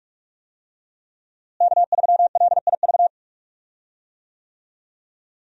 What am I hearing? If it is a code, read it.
Morse code, heard as a keyed tone, about 46 wpm, 700 Hz. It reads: K3LIV
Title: USA Call Signs - 5 Characters - Repeated 3x - 45wpm